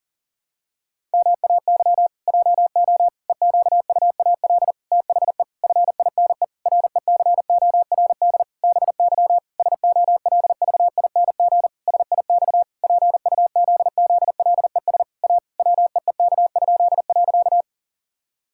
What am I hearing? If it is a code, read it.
Morse code, heard as a keyed tone, about 40 wpm, 700 Hz. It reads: MAY JO EQUAL THE FINE RECORD BY SOLVING SIX PUZZLES A WEEK?.
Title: Pangram Challenge v2 - 40wpm